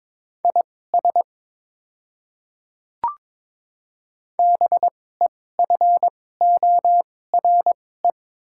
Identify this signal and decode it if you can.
Morse code, heard as a keyed tone, about 22 wpm, spaced as for 16 wpm, 700 Hz. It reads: IS BEFORE
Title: 2x Character Spacing - Top 200 Words - 22wpm